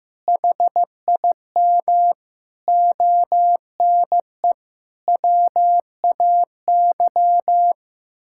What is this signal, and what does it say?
Morse code, heard as a keyed tone, about 15 wpm, 700 Hz. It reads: HIM ONE WAY
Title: Sentences from Top 1000 Words - Review Part 6 - 15wpm